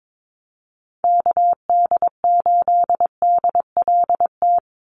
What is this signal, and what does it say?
Morse code, heard as a keyed tone, about 22 wpm, 700 Hz. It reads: KD8DLT
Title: USA Call Signs 22wpm